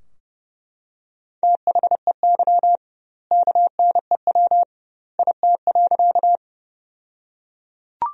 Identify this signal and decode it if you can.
Morse code, heard as a keyed tone, about 30 wpm, 700 Hz. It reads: THEY KNEW IT.
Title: 2x Word-Spacing - Sentences from Top 400 Words - 30wpm